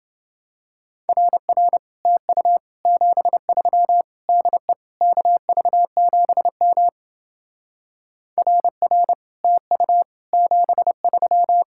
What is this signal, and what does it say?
Morse code, heard as a keyed tone, about 29 wpm, 700 Hz. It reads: RR TU 73 DE KV7M RR TU 73